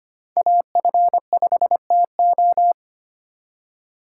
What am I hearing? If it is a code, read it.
Morse code, heard as a keyed tone, about 31 wpm, 700 Hz. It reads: AF5TO